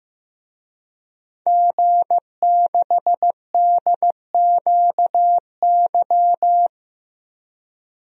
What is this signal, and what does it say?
Morse code, heard as a keyed tone, about 15 wpm, 700 Hz. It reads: G6DQY
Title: DXCC-100 Call Signs - 5 Characters - Repeated 2x - 15wpm